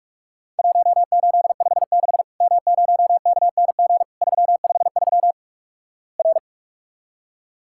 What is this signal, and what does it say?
Morse code, heard as a keyed tone, about 46 wpm, 700 Hz. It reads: 1856 M0KNG 353 R
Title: Worked All Europe DX Contest - QTC Exchange - 45wpm